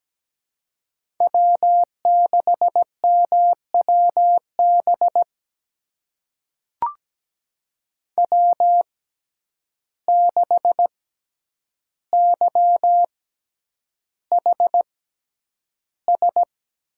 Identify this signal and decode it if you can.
Morse code, heard as a keyed tone, about 17 wpm, spaced as for 9 wpm, 700 Hz. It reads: W6MWB W6YHS
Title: USA Call Signs - 5 Characters - 6x Character Spacing - 17wpm